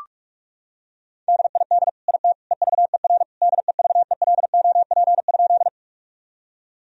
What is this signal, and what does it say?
Morse code, heard as a keyed tone, about 41 wpm, 700 Hz. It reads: DID IT EVER DEVELOP?
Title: Sentences from Top 1000 Words - Review Part 2 - 45wpm